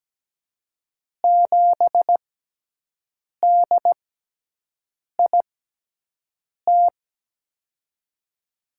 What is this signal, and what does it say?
Morse code, heard as a keyed tone, about 17 wpm, spaced as for 6 wpm, 700 Hz. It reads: 7DIT
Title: USA Call Signs - 5 Characters - 6x Character Spacing - 17wpm